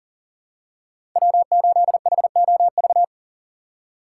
Morse code, heard as a keyed tone, about 37 wpm, 700 Hz. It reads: W8HOV
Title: USA Call Signs - 5 Characters - 2x Character Spacing - 40wpm